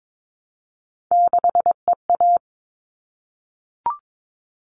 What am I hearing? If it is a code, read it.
Morse code, heard as a keyed tone, about 22 wpm, 700 Hz. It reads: 6EA
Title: All Cut Numbers - 3-Digit 22wpm